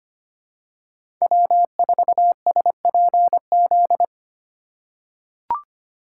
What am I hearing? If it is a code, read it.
Morse code, heard as a keyed tone, about 25 wpm, 700 Hz. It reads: W4SPZ
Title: USA Call Signs - 5 Characters - 4x Character Spacing - 25wpm